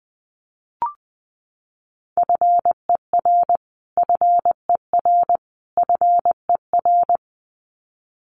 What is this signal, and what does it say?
Morse code, heard as a keyed tone, about 20 wpm, 700 Hz. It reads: FER FER FER
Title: Abbreviations Repeated 3x - 20wpm